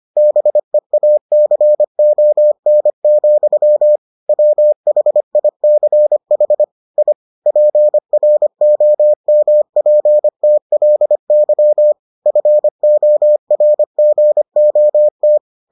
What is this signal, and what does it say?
Morse code, heard as a keyed tone, about 25 wpm, 600 Hz. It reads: BEACON, WHICH I PROMPTLY FORGOT